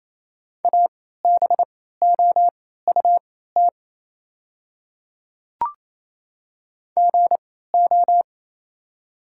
Morse code, heard as a keyed tone, about 27 wpm, spaced as for 16 wpm, 700 Hz. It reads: ABOUT GO
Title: 3x Character Spacing - Top 200 Words - 28wpm